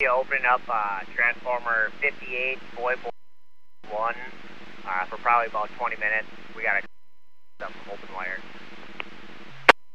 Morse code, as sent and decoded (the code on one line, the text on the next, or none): none